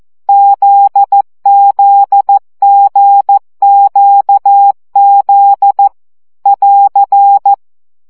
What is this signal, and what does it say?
Morse code, heard as a keyed tone, about 14 wpm, 800 Hz. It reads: ZZGQZ <AR>
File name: CW gyakorlas g z q 12 baud